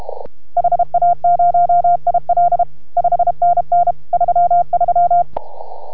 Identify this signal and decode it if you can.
Morse code, heard as a keyed tone, about 32 wpm, 690 Hz. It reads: HA0IL 5NN 33